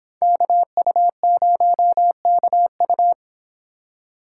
Morse code, heard as a keyed tone, about 26 wpm, 700 Hz. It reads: KU0KU
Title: Speed Racing - USA 2x2 Call Signs 22wpm